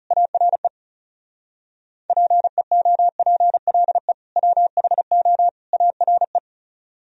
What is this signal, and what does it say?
Morse code, heard as a keyed tone, about 35 wpm, 700 Hz. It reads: ARE PEOPLE WHO ARE
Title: Speed-Racing - Sets of 3 Words 30wpm